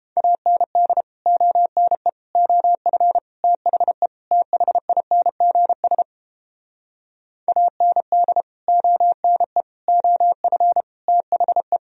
Morse code, heard as a keyed tone, about 29 wpm, 700 Hz. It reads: AND ONE OF THE THINGS AND ONE OF THE